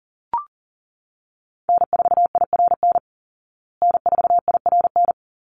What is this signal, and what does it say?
Morse code, heard as a keyed tone, about 40 wpm, 700 Hz. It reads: N4IRN N4IRN
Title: Call Signs Repeated 3x - 40wpm